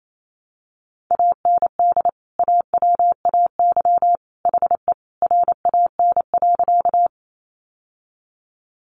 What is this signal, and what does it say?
Morse code, heard as a keyed tone, about 28 wpm, 700 Hz. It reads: AND AWAY HE RAN.
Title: Sentences from Top 600 Words 28wpm